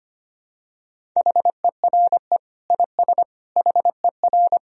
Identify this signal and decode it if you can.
Morse code, heard as a keyed tone, about 25 wpm, 700 Hz. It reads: HERE IS HER
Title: Sentences from Top 1000 Words - Review Part 1 - 25wpm